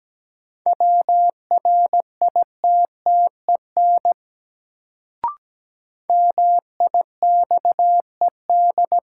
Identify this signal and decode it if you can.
Morse code, heard as a keyed tone, about 17 wpm, 700 Hz. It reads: WRITTEN MIXED
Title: Sentences from Top 1500 Words - New Word Drill - 17wpm